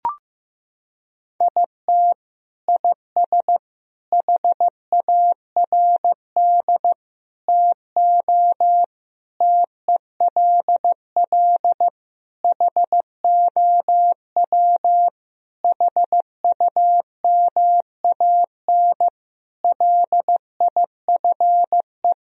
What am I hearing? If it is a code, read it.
Morse code, heard as a keyed tone, about 15 wpm, 700 Hz. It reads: IT IS HARD TO TELL HOW HUMAN LIFE
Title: Sentences from Top 1000 Words - Review Part 3 - 15wpm